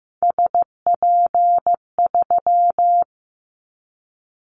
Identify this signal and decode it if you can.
Morse code, heard as a keyed tone, about 15 wpm, 700 Hz. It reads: SP3
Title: Callsign Prefixes for Top 25 DXCC - Repeated 3x - 15wpm